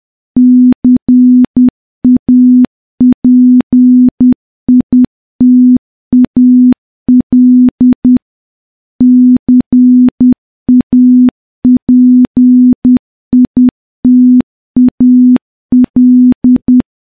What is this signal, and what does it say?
Morse code, heard as a keyed tone, about 10 wpm, 250 Hz. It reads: CAPITAL CAPITAL
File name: morse-training-10WPM